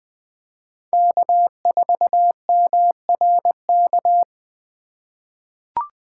Morse code, heard as a keyed tone, about 20 wpm, 700 Hz. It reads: K4MRK
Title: USA Call Signs - 5 Characters - 2x Character Spacing - 20wpm